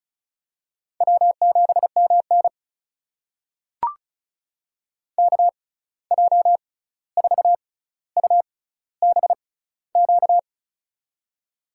Morse code, heard as a keyed tone, about 36 wpm, spaced as for 16 wpm, 700 Hz. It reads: W7MN KJ4UBQ
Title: USA Call Signs - 6x Character Spacing - 35wpm